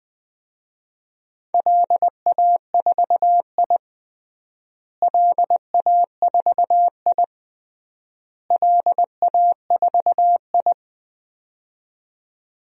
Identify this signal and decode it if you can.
Morse code, heard as a keyed tone, about 20 wpm, 700 Hz. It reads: LA4I LA4I LA4I
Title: DXCC-100 Call Signs - 4 Characters - Repeated 3x - 20wpm